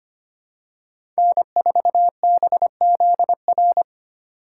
Morse code, heard as a keyed tone, about 25 wpm, 700 Hz. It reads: N4BZR
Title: USA Call Signs - 5 Characters - 2x Character Spacing - 25wpm